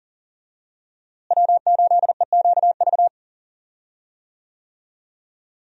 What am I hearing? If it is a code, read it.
Morse code, heard as a keyed tone, about 40 wpm, 700 Hz. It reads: W8EQV
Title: USA Call Signs - 5 Characters - Repeated 3x - 40wpm